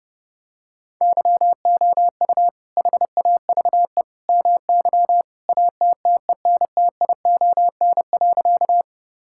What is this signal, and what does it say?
Morse code, heard as a keyed tone, about 30 wpm, 700 Hz. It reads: YOU HAVE MY ATTENTION.
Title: Sentences from Top 1000 Words - Review Part 6 - 30wpm